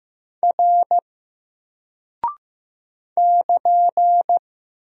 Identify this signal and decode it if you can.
Morse code, heard as a keyed tone, about 15 wpm, 700 Hz. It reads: R <KN>
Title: Single Character (Letter:Number:Punctuation:Prosign) 15wpm